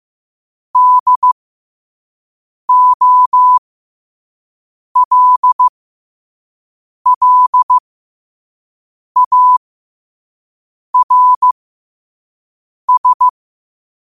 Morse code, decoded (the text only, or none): DOLLARS